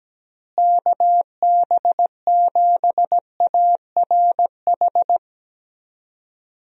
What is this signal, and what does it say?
Morse code, heard as a keyed tone, about 17 wpm, 700 Hz. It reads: KB7ARH
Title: USA Call Signs - 6 Characters - Repeated 3x - 17wpm